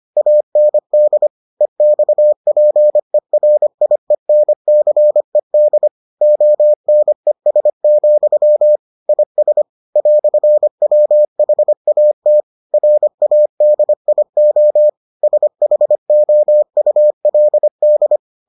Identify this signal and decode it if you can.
Morse code, heard as a keyed tone, about 25 wpm, 600 Hz. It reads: AND EXPERIENCED ONES, IS "WHAT RADIO SHOULD